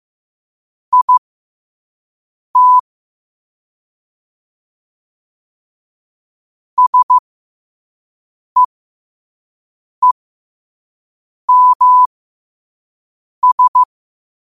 Morse code, heard as a keyed tone, about 15 wpm, spaced as for 5 wpm, 1000 Hz. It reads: IT SEEMS
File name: qotdfiles / 5wpm